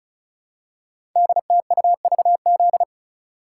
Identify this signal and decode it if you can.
Morse code, heard as a keyed tone, about 36 wpm, 700 Hz. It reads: DTUVZ
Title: Code Groups - 5 Letters - 35wpm